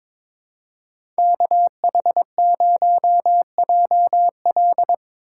Morse code, heard as a keyed tone, about 22 wpm, 700 Hz. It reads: KH0JL
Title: USA Call Signs - 5 Characters - 4x Character Spacing - 22wpm